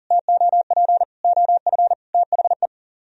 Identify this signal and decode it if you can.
Morse code, heard as a keyed tone, about 55 wpm, spaced as for 48 wpm, 700 Hz. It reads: TOP OF THE
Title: Speed-Racing - Sets of 4 Words 40wpm